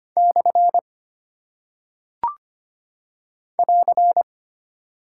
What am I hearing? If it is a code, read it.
Morse code, heard as a keyed tone, about 25 wpm, 700 Hz. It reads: / <AR>
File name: Single Punctuation-Prosign - 25wpm